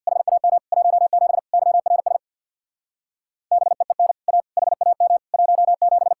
Morse code, heard as a keyed tone, about 62 wpm, 700 Hz. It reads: HAM 17 YRS BEEN A HAM 17
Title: Speed-Racing Common QSO Phrases - 50wpm